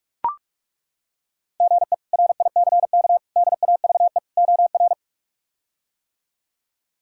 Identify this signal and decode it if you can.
Morse code, heard as a keyed tone, about 41 wpm, 700 Hz. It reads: GE RICK DAVE OR
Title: K1USN SST QSO Practice - 45wpm